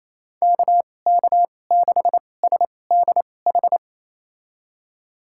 Morse code, heard as a keyed tone, about 28 wpm, spaced as for 20 wpm, 700 Hz. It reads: KK6SDH
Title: USA Call Signs - 6 Characters - 2x Character Spacing - 28wpm